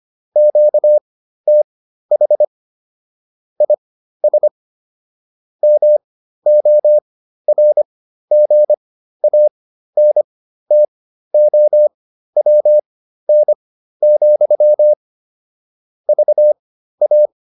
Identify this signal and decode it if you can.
Morse code, heard as a keyed tone, about 25 wpm, spaced as for 13 wpm, 600 Hz. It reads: QTH IS MORGANTOWN, VA